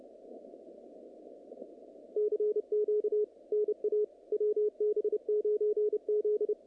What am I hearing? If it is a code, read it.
Morse code, heard as a keyed tone, about 29 wpm, 440 Hz, with radio noise. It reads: CQ NA WB9Z
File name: contests / naqp-cw-k5tr-op-k5pi- -2200